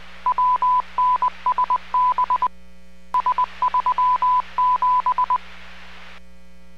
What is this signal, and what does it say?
Morse code, heard as a keyed tone, about 20 wpm, 1000 Hz. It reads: WNSB S37